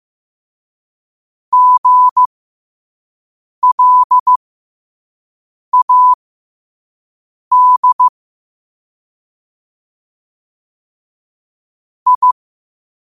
Morse code, heard as a keyed tone, about 15 wpm, spaced as for 5 wpm, 1000 Hz. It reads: GLAD I
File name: qotdfiles / 5wpm